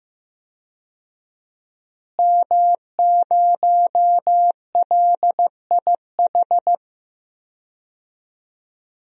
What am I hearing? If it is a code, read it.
Morse code, heard as a keyed tone, about 15 wpm, 700 Hz. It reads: M0LIH
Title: DXCC-100 Call Signs - 5 Characters - Repeated 2x - 15wpm